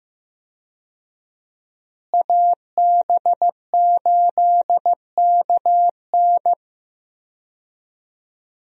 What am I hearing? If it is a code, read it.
Morse code, heard as a keyed tone, about 15 wpm, 700 Hz. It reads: AB8KN
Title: USA Call Signs - 5 Characters - Repeated 2x - 15wpm